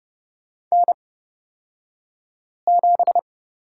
Morse code, heard as a keyed tone, about 30 wpm, spaced as for 5 wpm, 700 Hz. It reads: N7
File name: Lesson 36d - BT - Two Characters 30wpm